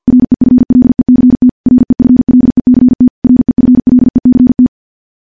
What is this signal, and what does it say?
Morse code, heard as a keyed tone, about 50 wpm, 250 Hz. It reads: REFLECT REFLECT REFLECT
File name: morse / morse-training-50WPM